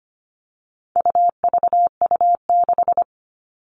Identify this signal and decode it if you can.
Morse code, heard as a keyed tone, about 25 wpm, 700 Hz. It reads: UVU6
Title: All Cut Numbers - 4-Digit 25wpm